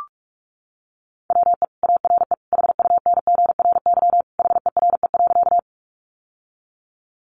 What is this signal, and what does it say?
Morse code, heard as a keyed tone, about 44 wpm, 700 Hz. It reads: WE ARE HUNGRY HERE.
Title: Sentences from Top 1400 Words 45wpm